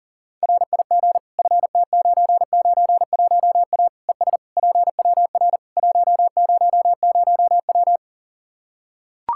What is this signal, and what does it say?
Morse code, heard as a keyed tone, about 40 wpm, 700 Hz. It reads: RIG FT991A ES PWR 100W
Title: Practice QSO - Exchange 2 - Info - 40wpm